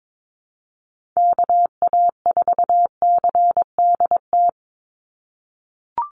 Morse code, heard as a keyed tone, about 22 wpm, 700 Hz. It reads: KA4CDT
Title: USA Call Signs 22wpm